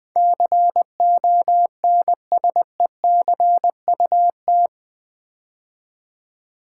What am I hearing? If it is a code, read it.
Morse code, heard as a keyed tone, about 20 wpm, 700 Hz. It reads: CONSECUT
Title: Build A Word - 4 Letters At A Time - 11-Letter Words - 20wpm